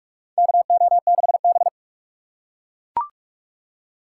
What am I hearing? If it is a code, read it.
Morse code, heard as a keyed tone, about 43 wpm, 700 Hz. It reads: KO6B
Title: USA Call Signs - 4 Characters - 3x Character Spacing - 45wpm